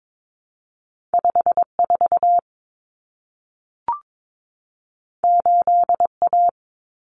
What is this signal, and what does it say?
Morse code, heard as a keyed tone, about 22 wpm, 700 Hz. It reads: 54 8A
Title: Common Cut Numbers - 2-Digit 22wpm